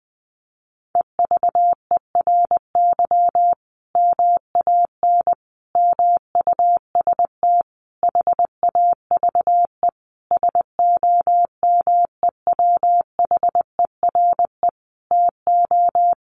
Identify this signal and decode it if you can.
Morse code, heard as a keyed tone, about 20 wpm, 700 Hz. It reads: EVERY MAN MUST HAVE SOMEWHERE TO
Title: Sentences from Top 1400 Words 20wpm